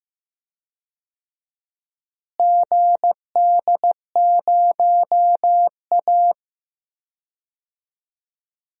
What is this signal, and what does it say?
Morse code, heard as a keyed tone, about 15 wpm, 700 Hz. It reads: GD0A
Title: DXCC-100 Call Signs - 4 Characters - Repeated 3x - 15wpm